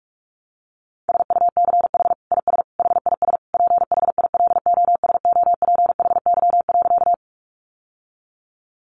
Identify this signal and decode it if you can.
Morse code, heard as a keyed tone, about 44 wpm, 700 Hz. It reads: SUCH IS HIS PHILOSOPHY.
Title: Sentences from Top 900 Words 45wpm